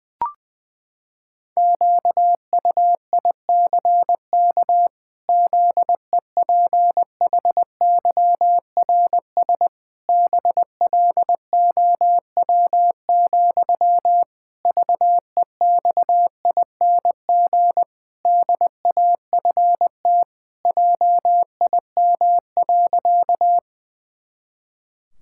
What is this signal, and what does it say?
Morse code, heard as a keyed tone, about 20 wpm, 700 Hz. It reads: QUICK ZEPHYRS BLOW, VEXING DAFT JIM.
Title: Pangram Challenge v2 - 20wpm